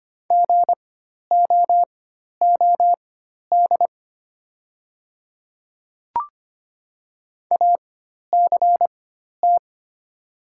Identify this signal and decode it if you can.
Morse code, heard as a keyed tone, about 25 wpm, spaced as for 12 wpm, 700 Hz. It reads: GOOD ACT
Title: 4x Character Spacing - Top 200 Words - 25wpm